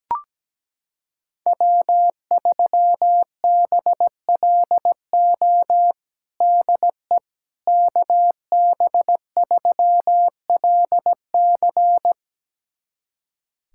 Morse code, heard as a keyed tone, about 17 wpm, 700 Hz. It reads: W3BLO DE KB3LC
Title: USA Callsign DE Callsign - 5 Characters - 17wpm